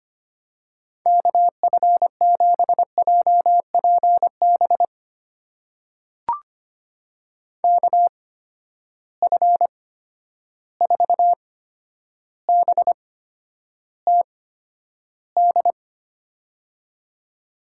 Morse code, heard as a keyed tone, about 25 wpm, spaced as for 11 wpm, 700 Hz. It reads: KF7JPB KF4BTD
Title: USA Call Signs - 6 Characters - 8x Character Spacing - 25wpm